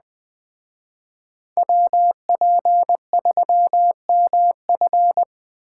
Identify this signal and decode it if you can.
Morse code, heard as a keyed tone, about 20 wpm, 700 Hz. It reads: WP3MF
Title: Speed Racing - USA 2x2 Call Signs 17wpm